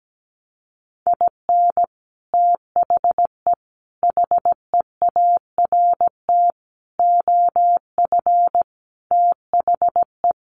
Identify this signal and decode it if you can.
Morse code, heard as a keyed tone, about 17 wpm, 700 Hz. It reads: IN THE HEART OF THE
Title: Sets of 5 Words 17wpm